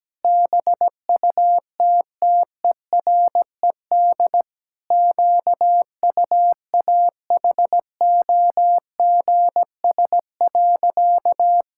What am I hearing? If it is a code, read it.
Morse code, heard as a keyed tone, about 17 wpm, 700 Hz. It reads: BUTTERED QUAHOGS.
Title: Pangram Challenge v2 - 17wpm